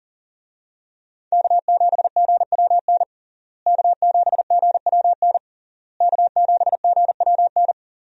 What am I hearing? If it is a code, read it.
Morse code, heard as a keyed tone, about 40 wpm, 700 Hz. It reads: K7GWN K7GWN K7GWN
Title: USA Call Signs - 5 Characters - Repeated 3x - 40wpm